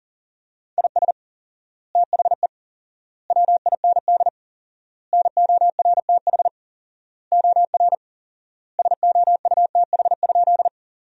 Morse code, heard as a keyed tone, about 40 wpm, 700 Hz. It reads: IS THE WIND NORTH OR SOUTH?
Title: 4x Word-Spacing - Sentences from Top 400 Words - 40wpm